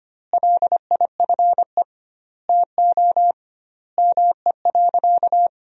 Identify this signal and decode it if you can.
Morse code, heard as a keyed tone, about 25 wpm, 700 Hz. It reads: LIFE TO ME.
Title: 2x Word-Spacing - Sentences from Top 500 Words - 25wpm